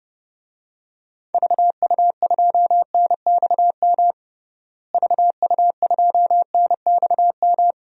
Morse code, heard as a keyed tone, about 29 wpm, 700 Hz. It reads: VU2NXM VU2NXM
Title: DXCC-100 Call Signs - 6 Characters - Repeated 3x - 30wpm